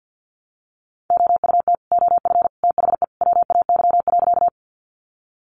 Morse code, heard as a keyed tone, about 50 wpm, 700 Hz. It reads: OUT OF THE WAY.
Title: Sentences from Top 100 Words 50wpm